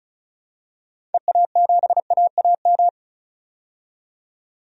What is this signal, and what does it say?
Morse code, heard as a keyed tone, about 36 wpm, 700 Hz. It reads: EA7AAM
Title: DXCC-100 Call Signs - 6 Characters - 35wpm